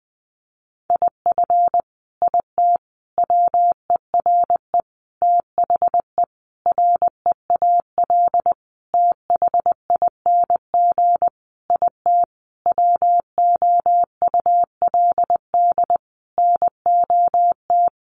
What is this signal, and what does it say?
Morse code, heard as a keyed tone, about 20 wpm, 700 Hz. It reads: IF IT WERE THE REAL THING IT WOULD NOT
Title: Sentences from Top 1300 Words 20wpm